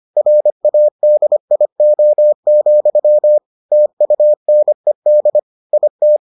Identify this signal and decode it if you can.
Morse code, heard as a keyed tone, about 25 wpm, 600 Hz. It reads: RADIO, TUNED IT